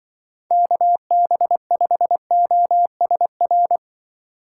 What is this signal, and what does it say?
Morse code, heard as a keyed tone, about 25 wpm, 700 Hz. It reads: KB5OSR